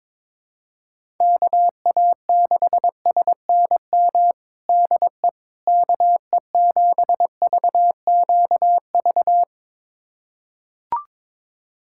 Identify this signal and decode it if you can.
Morse code, heard as a keyed tone, about 22 wpm, 700 Hz. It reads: KA6SNM DE KE7VQV
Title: USA Callsign DE Callsign - 6 Characters - 22wpm